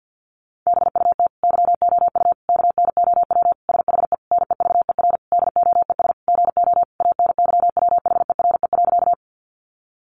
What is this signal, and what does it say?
Morse code, heard as a keyed tone, about 50 wpm, 700 Hz. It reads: BUT YOU KNOW SHE NEVER DOES GO ANYWHERE.
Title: Sentences from Top 1400 Words 50wpm